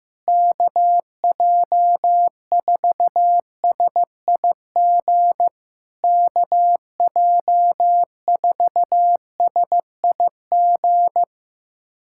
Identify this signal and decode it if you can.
Morse code, heard as a keyed tone, about 16 wpm, 700 Hz. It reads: KJ4SIG KJ4SIG
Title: SOTA QSO Practice 15wpm